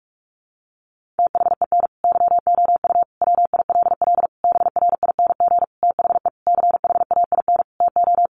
Morse code, heard as a keyed tone, about 44 wpm, 700 Hz. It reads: THEN YOU WILL BRING THE CHAIN TO
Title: Sentences from Top 1500 Words 45wpm